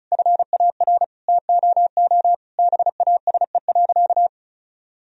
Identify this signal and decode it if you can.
Morse code, heard as a keyed tone, about 36 wpm, 700 Hz. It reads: FAR TOO BASE.
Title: Sentences from Top 1000 Words - Review Part 1 - 35wpm